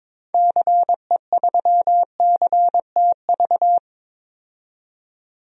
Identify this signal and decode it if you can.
Morse code, heard as a keyed tone, about 22 wpm, 700 Hz. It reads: CE3CTV
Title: DXCC-100 Call Signs - 6 Characters - 22wpm